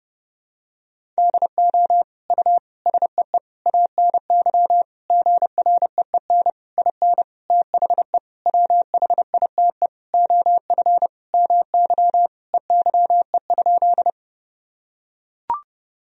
Morse code, heard as a keyed tone, about 30 wpm, 700 Hz. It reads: DO U SEE ANY GREEN IN THE WHITE OF MY EYE?
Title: Sentences from Top 500 Words and Abbreviations 30wpm